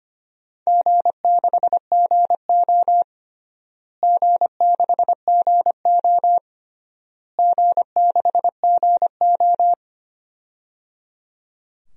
Morse code, heard as a keyed tone, about 25 wpm, 700 Hz. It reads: G6GO G6GO G6GO
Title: DXCC-100 Call Signs - 4 Characters - Repeated 3x - 25wpm